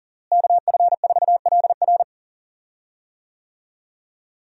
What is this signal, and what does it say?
Morse code, heard as a keyed tone, about 37 wpm, 700 Hz. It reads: KF4LR
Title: USA Call Signs - 5 Characters - Repeated 3x - 40wpm